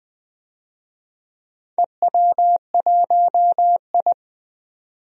Morse code, heard as a keyed tone, about 20 wpm, 700 Hz. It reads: EW1I
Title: DXCC-100 Call Signs - 4 Characters - 20wpm